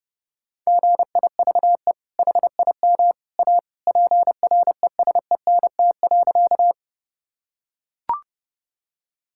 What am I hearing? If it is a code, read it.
Morse code, heard as a keyed tone, about 30 wpm, 700 Hz. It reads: GIVE HIM A PRESENT.
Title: Sentences from Top 1000 Words - Review Part 5 - 30wpm